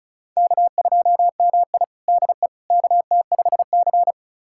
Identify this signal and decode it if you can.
Morse code, heard as a keyed tone, about 36 wpm, 700 Hz. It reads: K2MI DE KT5C
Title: USA Callsign DE Callsign - 4 Characters - 35wpm